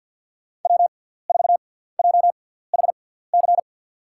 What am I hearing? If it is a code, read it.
Morse code, heard as a keyed tone, about 53 wpm, spaced as for 17 wpm, 700 Hz. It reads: W4JHC
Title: USA Call Signs - 5 Characters - 6x Character Spacing - 50wpm